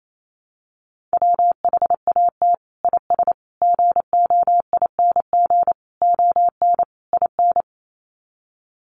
Morse code, heard as a keyed tone, about 28 wpm, 700 Hz. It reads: WHAT IS GOING ON IN
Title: Sets of 5 Words Encore2 28wpm